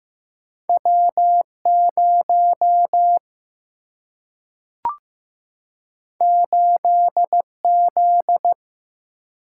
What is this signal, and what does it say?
Morse code, heard as a keyed tone, about 15 wpm, 700 Hz. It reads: W0 8Z
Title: Single-Letter Single-Number 15wpm